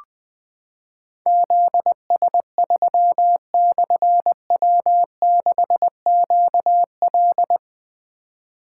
Morse code, heard as a keyed tone, about 20 wpm, 700 Hz. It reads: ZS3/W6QL
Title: Difficult Call Signs 20wpm